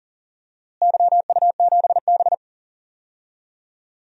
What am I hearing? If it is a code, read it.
Morse code, heard as a keyed tone, about 40 wpm, 700 Hz. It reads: YU7B
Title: DXCC-100 Call Signs - 4 Characters - Repeated 2x - 40wpm